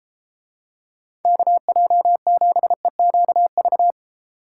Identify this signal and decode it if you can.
Morse code, heard as a keyed tone, about 33 wpm, 700 Hz. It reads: KJ7EQV